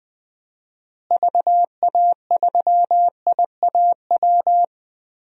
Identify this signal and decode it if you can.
Morse code, heard as a keyed tone, about 20 wpm, 700 Hz. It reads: VA3IAW
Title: DXCC-100 Call Signs - 6 Characters - Repeated 3x - 20wpm